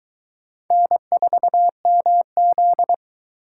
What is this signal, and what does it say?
Morse code, heard as a keyed tone, about 22 wpm, 700 Hz. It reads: N4MZ